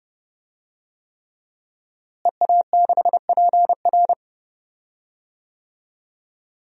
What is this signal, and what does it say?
Morse code, heard as a keyed tone, about 32 wpm, 700 Hz. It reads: EA6PR
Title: DXCC-100 Call Signs - 5 Characters - Repeated 3x - 30wpm